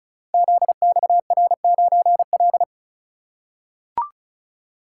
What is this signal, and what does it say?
Morse code, heard as a keyed tone, about 36 wpm, 700 Hz. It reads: ZXR9L
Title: Code Groups - 5 Letter:Numbers - 35wpm